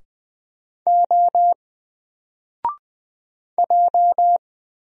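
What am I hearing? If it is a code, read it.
Morse code, heard as a keyed tone, about 20 wpm, 700 Hz. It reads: O J
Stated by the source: Single letter and number Morse drill at 20 wpm, from N4LCD.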